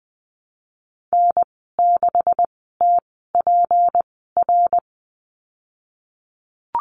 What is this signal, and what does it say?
Morse code, heard as a keyed tone, about 19 wpm, spaced as for 14 wpm, 700 Hz. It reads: N6TPR
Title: 193 Lesson 31g - Letter v - Call Signs 20wpm